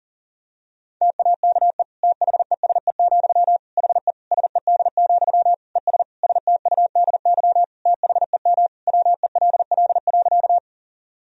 Morse code, heard as a keyed tone, about 40 wpm, 700 Hz. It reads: TAKE THESE, HE SED, ES STUDY THEM WELL.
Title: Sentences from Top 300 Words and Abbreviations 40wpm